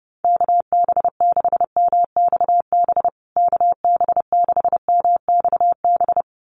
Morse code, heard as a keyed tone, about 30 wpm, 700 Hz. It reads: KB6MXB KB6MXB
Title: USA Call Signs Repeated 2x 30wpm